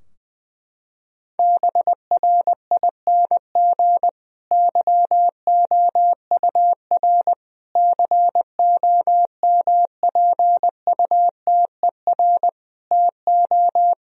Morse code, heard as a keyed tone, about 21 wpm, 700 Hz. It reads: BRING YOUR COMPUTER TO
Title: Sentences from Top 1000 Words - Review Part 3 - 20wpm